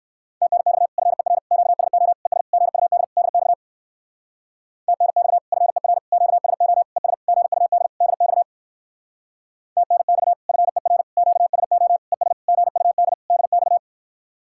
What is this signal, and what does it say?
Morse code, heard as a keyed tone, about 67 wpm, 700 Hz. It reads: TNX FER QSO ES GUD DX TNX FER QSO ES GUD DX TNX FER QSO ES GUD DX